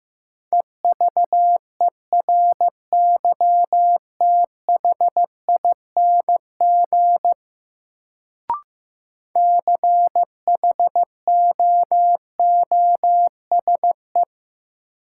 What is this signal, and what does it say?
Morse code, heard as a keyed tone, about 15 wpm, 700 Hz. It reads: EVERYTHING CHOOSE
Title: Sentences from Top 700 Words - New Word Drill - 15wpm